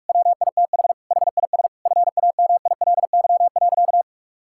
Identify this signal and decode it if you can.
Morse code, heard as a keyed tone, about 46 wpm, 700 Hz. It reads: WITH HIS FAMILY.
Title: Sentences from Top 1600 Words 45wpm